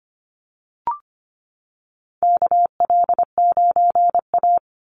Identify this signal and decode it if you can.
Morse code, heard as a keyed tone, about 25 wpm, 700 Hz. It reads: KL9A
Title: Call Signs - 25wpm